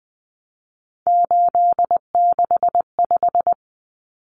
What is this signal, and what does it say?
Morse code, heard as a keyed tone, about 20 wpm, 700 Hz. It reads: 865
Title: Common Cut Numbers - 3-Digit 20wpm